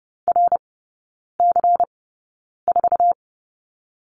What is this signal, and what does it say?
Morse code, heard as a keyed tone, about 30 wpm, spaced as for 9 wpm, 690 Hz. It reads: RC4
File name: Lesson 16e - Letter c - Three Characters 30wpm